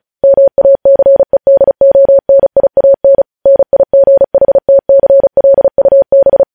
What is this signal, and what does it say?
Morse code, heard as a keyed tone, about 36 wpm, 550 Hz. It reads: MACEDONIAN NIGHTCLUB
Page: News Headlines in Morse Code at 35 WPM